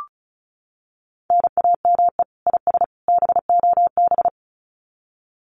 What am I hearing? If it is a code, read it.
Morse code, heard as a keyed tone, about 34 wpm, 700 Hz. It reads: NAME IS BOB
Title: Common QSO Phrases 35wpm